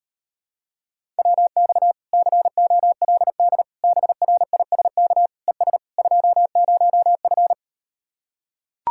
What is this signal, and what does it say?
Morse code, heard as a keyed tone, about 38 wpm, 700 Hz. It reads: WX COLD BRISK ES 20F